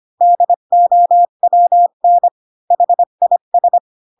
Morse code, heard as a keyed tone, about 25 wpm, 700 Hz. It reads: DOWN HIS
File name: Pinocchio / Pinocchio ch7 20 25 ik1hge